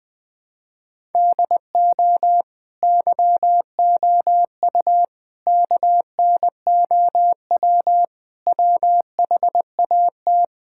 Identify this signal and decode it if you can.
Morse code, heard as a keyed tone, about 20 wpm, 700 Hz. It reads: DO YOU KNOW WHAT
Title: Sentences from Top 1000 Words - Review Part 5 - 20wpm